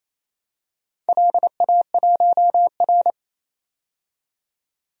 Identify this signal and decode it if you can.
Morse code, heard as a keyed tone, about 29 wpm, 700 Hz. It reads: LA1R
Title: DXCC-100 Call Signs - 4 Characters - 28wpm